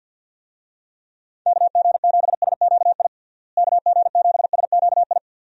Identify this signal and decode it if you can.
Morse code, heard as a keyed tone, about 50 wpm, 700 Hz. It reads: KG7SQI KG7SQI
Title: USA Call Signs - 6 Characters - Repeated 3x - 50wpm